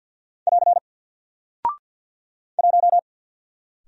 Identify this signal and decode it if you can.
Morse code, heard as a keyed tone, about 50 wpm, 700 Hz. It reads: <AR> 1
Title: Single Character (Letter:Number:Punctuation:Prosign) 50wpm